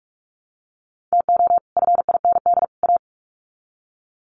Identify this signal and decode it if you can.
Morse code, heard as a keyed tone, about 44 wpm, 700 Hz. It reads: TO FIND A
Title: Sets of 3 Words 45wpm